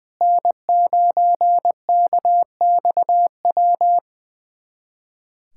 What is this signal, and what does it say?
Morse code, heard as a keyed tone, about 20 wpm, 700 Hz. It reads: N9KXW
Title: USA Callsign DE Callsign - 5 Characters - 20wpm